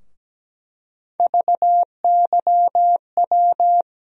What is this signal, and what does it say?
Morse code, heard as a keyed tone, about 17 wpm, 700 Hz. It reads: VYW
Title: USA Call Sign Postfixes - 3 Letters 17wpm